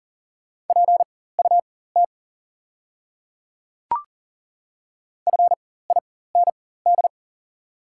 Morse code, heard as a keyed tone, about 40 wpm, spaced as for 19 wpm, 700 Hz. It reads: PUT FIND
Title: 4x Character Spacing - Top 300 Words - 40wpm